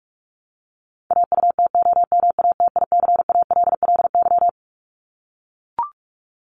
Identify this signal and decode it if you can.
Morse code, heard as a keyed tone, about 44 wpm, 700 Hz. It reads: AUTOMATICALLY
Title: Top 1751-2000 Words 45wpm